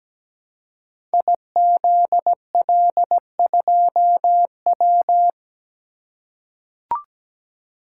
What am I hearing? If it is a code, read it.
Morse code, heard as a keyed tone, about 17 wpm, 700 Hz. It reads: IZL2W
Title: Code Groups - 5 Letter:Numbers - 17wpm